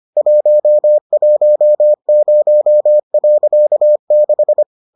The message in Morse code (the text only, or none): 110.6